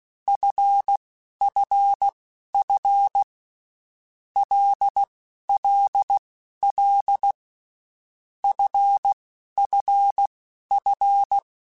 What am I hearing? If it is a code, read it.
Morse code, heard as a keyed tone, about 16 wpm, spaced as for 11 wpm, 800 Hz. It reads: FFF LLL FFF
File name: Groups@10 / Groups@10 mp3s / LFCYQ@10